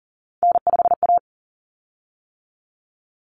Morse code, heard as a keyed tone, about 40 wpm, 700 Hz. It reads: N5A
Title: Common Cut Numbers - 3-Digit 40wpm